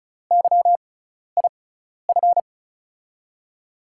Morse code, heard as a keyed tone, about 36 wpm, spaced as for 12 wpm, 700 Hz. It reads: YIF
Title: USA Call Signs - 5 Characters - 6x Character Spacing - 35wpm